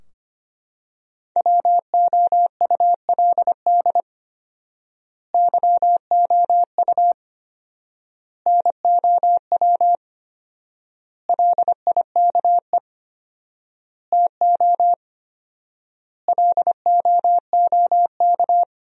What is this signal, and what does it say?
Morse code, heard as a keyed tone, about 25 wpm, 700 Hz. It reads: WOULD YOU NOW LIKE TO LOOK
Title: 4x Word-Spacing - Sentences from Top 200 Words - 25wpm